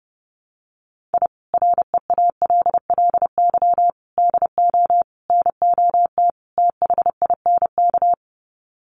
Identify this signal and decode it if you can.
Morse code, heard as a keyed tone, about 30 wpm, 700 Hz. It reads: I REALLY DO NOT THINK
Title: Sets of 5 Words Encore2 30wpm